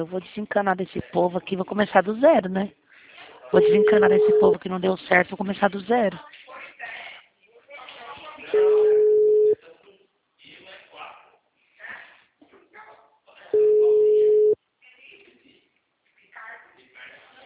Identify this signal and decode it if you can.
Ringback tone (Europe, 425 Hz), with speech.